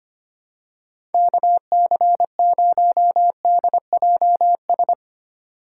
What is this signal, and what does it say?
Morse code, heard as a keyed tone, about 25 wpm, 700 Hz. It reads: KC0DJS
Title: USA Call Signs - 6 Characters - 3x Character Spacing - 25wpm